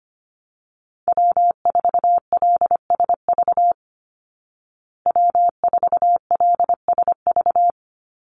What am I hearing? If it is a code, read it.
Morse code, heard as a keyed tone, about 25 wpm, 700 Hz. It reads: W4LSV W4LSV
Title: Call Signs Repeated 2x - 25wpm